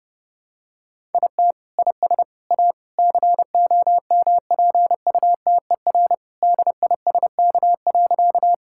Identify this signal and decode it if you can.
Morse code, heard as a keyed tone, about 30 wpm, 700 Hz. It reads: IT IS A COMPUTER DISK.
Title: Sentences from Top 1000 Words - Review Part 6 - 30wpm